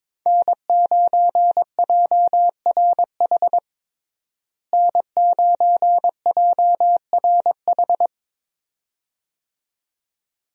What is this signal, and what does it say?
Morse code, heard as a keyed tone, about 22 wpm, 700 Hz. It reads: N9JRH N9JRH
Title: USA Call Signs - 5 Characters - Repeated 2x - 22wpm